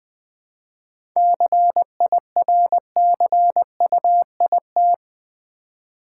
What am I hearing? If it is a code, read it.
Morse code, heard as a keyed tone, about 20 wpm, 700 Hz. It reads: CIRCUIT
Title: 7 Letter Words - First 3 Letters Given - 20wpm